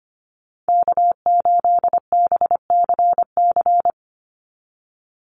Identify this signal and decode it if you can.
Morse code, heard as a keyed tone, about 25 wpm, 690 Hz. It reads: K8BCC
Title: USA Call Signs 25wpm